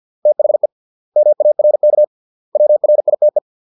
Morse code, heard as a keyed tone, about 50 wpm, 600 Hz. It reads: THE MARK WRITE